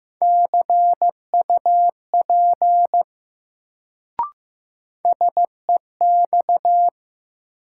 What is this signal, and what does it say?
Morse code, heard as a keyed tone, about 15 wpm, 700 Hz. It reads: CUP SEX
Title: Sentences from Top 1600 Words - New Word Drill - 15wpm